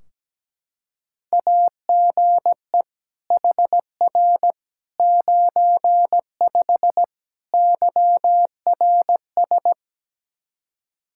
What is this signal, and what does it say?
Morse code, heard as a keyed tone, about 17 wpm, 700 Hz. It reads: AGE HR 95 YRS
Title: Practice QSO - Exchange 3 - More Info - 17wpm